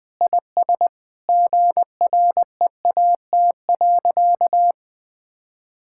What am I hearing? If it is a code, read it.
Morse code, heard as a keyed tone, about 20 wpm, 700 Hz. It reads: IS GREAT.
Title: Sentences with 4-Word Phrases - 20wpm